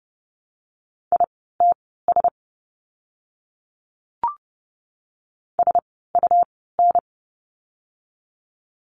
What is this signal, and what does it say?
Morse code, heard as a keyed tone, about 30 wpm, spaced as for 17 wpm, 700 Hz. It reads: ITS SUN
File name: Lesson 17f - Letter u - Words 30wpm